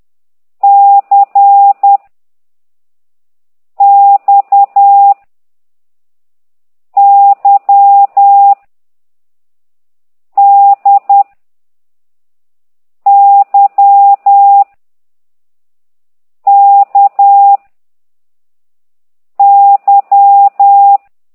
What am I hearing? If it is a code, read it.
Morse code, heard as a keyed tone, about 10 wpm, 800 Hz. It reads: C X Y D Y K Y
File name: exerc7